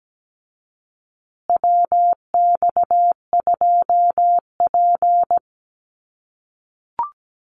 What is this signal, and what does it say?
Morse code, heard as a keyed tone, about 17 wpm, 700 Hz. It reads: WX2P
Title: USA Call Signs Repeated 3x 17wpm